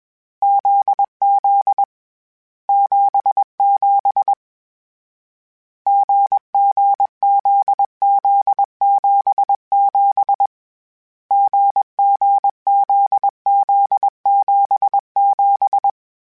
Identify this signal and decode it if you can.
Morse code, heard as a keyed tone, about 21 wpm, 800 Hz. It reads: ZZ 77 GG ZZ 77 GG ZZ 77
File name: CW Code Groups-mp3s / Sets mp3s / GZ GZ7 sp @20